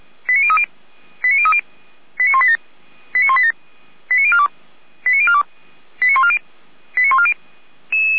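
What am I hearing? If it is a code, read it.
Five-tone selective call (ZVEI), heard as quick runs of five tones, tones of about 70 ms. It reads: 89030 89030 89187 89187 89042 89042 89140 89140
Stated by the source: ZVEI five-tone call received by SDR.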